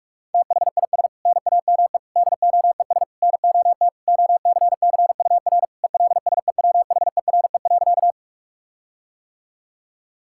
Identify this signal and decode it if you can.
Morse code, heard as a keyed tone, about 46 wpm, 700 Hz. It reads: THIS NAME DOES NOT OCCUR ELSEWHERE.
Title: Sentences from Top 1600 Words 45wpm